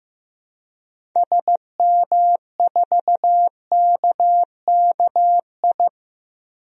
Morse code, heard as a keyed tone, about 15 wpm, 700 Hz. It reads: SM4KKI
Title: DXCC-100 Call Signs - 6 Characters - 15wpm